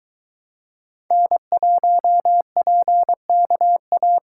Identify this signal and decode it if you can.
Morse code, heard as a keyed tone, about 23 wpm, 700 Hz. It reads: N1PKA